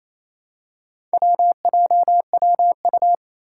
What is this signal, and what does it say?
Morse code, heard as a keyed tone, about 28 wpm, 700 Hz. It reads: WJWU
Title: Four Challenging Letter-Number Combinations 28wpm